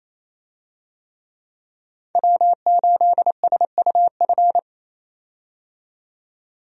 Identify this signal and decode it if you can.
Morse code, heard as a keyed tone, about 28 wpm, 700 Hz. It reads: W8SUF
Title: USA Call Signs - 5 Characters - 28wpm